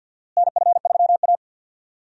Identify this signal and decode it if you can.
Morse code, heard as a keyed tone, about 47 wpm, 700 Hz. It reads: NF3A
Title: USA Call Signs - 4 Characters - Repeated 2x - 50wpm